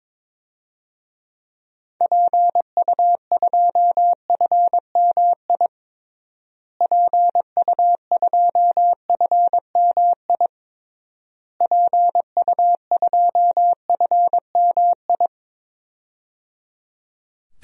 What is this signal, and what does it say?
Morse code, heard as a keyed tone, about 22 wpm, 700 Hz. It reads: PU2FMI PU2FMI PU2FMI
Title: DXCC-100 Call Signs - 6 Characters - Repeated 3x - 22wpm